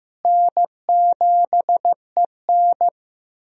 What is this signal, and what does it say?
Morse code, heard as a keyed tone, about 15 wpm, 700 Hz. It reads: N7EN
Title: USA Call Signs - 4 Characters - Repeated 3x - 15wpm